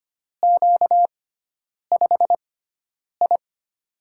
Morse code, heard as a keyed tone, about 25 wpm, spaced as for 9 wpm, 700 Hz. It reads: Q5I
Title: USA Call Signs - 4 Characters - 6x Character Spacing - 25wpm